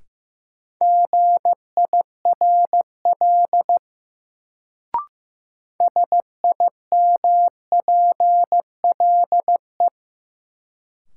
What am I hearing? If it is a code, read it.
Morse code, heard as a keyed tone, about 15 wpm, 700 Hz. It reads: GIRL SIMPLE
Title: Sentences from Top 400 Words - New Word Drill - 15wpm